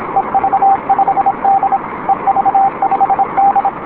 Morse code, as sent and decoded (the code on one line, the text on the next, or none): . ...- ..... -.. / . ...- ..... -..
EV5D EV5D